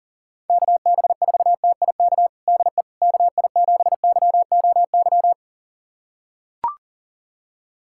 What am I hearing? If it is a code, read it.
Morse code, heard as a keyed tone, about 40 wpm, 690 Hz. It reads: KB4TIK DE KI7YOY
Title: USA Callsign DE Callsign - 6 Characters - 40wpm